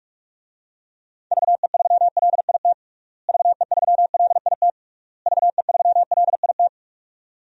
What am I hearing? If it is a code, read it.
Morse code, heard as a keyed tone, about 46 wpm, 700 Hz. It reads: VE3LIT VE3LIT VE3LIT
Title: DXCC-100 Call Signs - 6 Characters - Repeated 3x - 45wpm